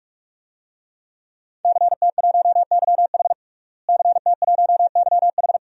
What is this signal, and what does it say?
Morse code, heard as a keyed tone, about 46 wpm, 700 Hz. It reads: CT1YH CT1YH
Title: DXCC-100 Call Signs - 5 Characters - Repeated 2x - 45wpm